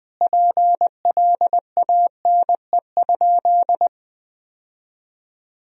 Morse code, heard as a keyed tone, about 21 wpm, 700 Hz. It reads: PLANE?
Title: Sentences from Top 1000 Words - Review Part 3 - 20wpm